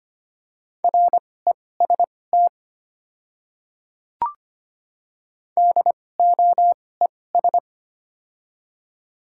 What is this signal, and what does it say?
Morse code, heard as a keyed tone, about 25 wpm, spaced as for 18 wpm, 700 Hz. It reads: REST DOES
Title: 2x Character Spacing - Top 500 Words - 25wpm